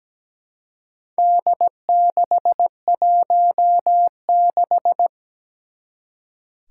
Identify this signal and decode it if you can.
Morse code, heard as a keyed tone, about 17 wpm, 700 Hz. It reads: D616
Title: Four Challenging Letter-Number Combinations 17wpm